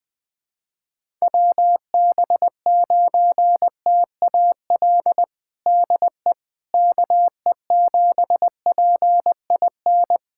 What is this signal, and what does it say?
Morse code, heard as a keyed tone, about 20 wpm, 700 Hz. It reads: WB9TAL DE KE7PIN
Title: USA Callsign DE Callsign - 6 Characters - 20wpm